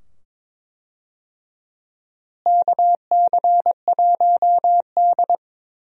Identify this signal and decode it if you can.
Morse code, heard as a keyed tone, about 22 wpm, 700 Hz. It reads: KC1D
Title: USA Call Signs - 4 Characters - Repeated 2x - 22wpm